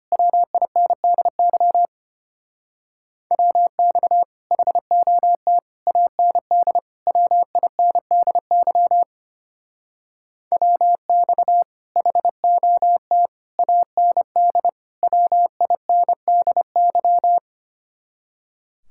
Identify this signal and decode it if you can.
Morse code, heard as a keyed tone, about 32 wpm, 700 Hz. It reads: WINDY WX HOT AND WINDY WX HOT AND WINDY